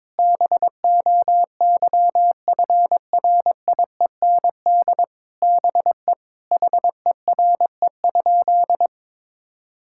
Morse code, heard as a keyed tone, about 22 wpm, 700 Hz. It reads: BOYFRIEND BE HERE?
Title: Sentences from Top 1000 Words - Review Part 5 - 22wpm